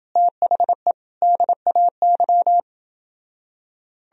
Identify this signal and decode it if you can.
Morse code, heard as a keyed tone, about 30 wpm, 700 Hz. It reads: THE DAY